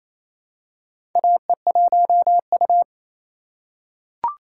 Morse code, heard as a keyed tone, about 28 wpm, 700 Hz. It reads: AE1U
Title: USA Call Signs - 4 Characters - 3x Character Spacing - 28wpm